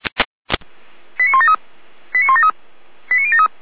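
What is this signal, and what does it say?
Five-tone selective call (ZVEI), heard as quick runs of five tones, tones of about 70 ms. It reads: 89173 89173 89073
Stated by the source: Received on SDR, ZVEI code 89320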